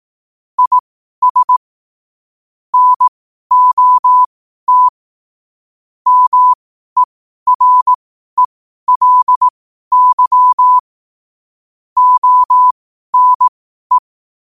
Morse code, heard as a keyed tone, about 18 wpm, spaced as for 12 wpm, 1000 Hz. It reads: IS NOT MERELY ONE
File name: qotdfiles / 12wpm